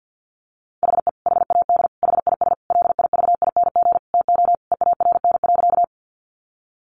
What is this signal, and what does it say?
Morse code, heard as a keyed tone, about 50 wpm, 700 Hz. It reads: HE HAD HIS LIVING TO EARN.
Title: Sentences from Top 1500 Words 50wpm